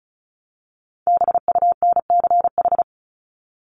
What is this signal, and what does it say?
Morse code, heard as a keyed tone, about 34 wpm, 700 Hz. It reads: BUNCH
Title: Top 1751-2000 Words 35wpm